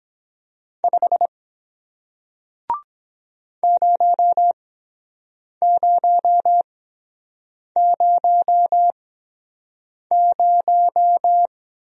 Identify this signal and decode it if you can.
Morse code, heard as a keyed tone, about 23 wpm, spaced as for 7 wpm, 700 Hz. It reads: 5 0000